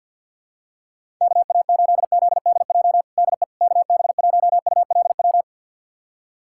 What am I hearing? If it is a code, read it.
Morse code, heard as a keyed tone, about 45 wpm, 700 Hz. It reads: KA8ZDJ DE KB1ULW
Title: USA Callsign DE Callsign 50wpm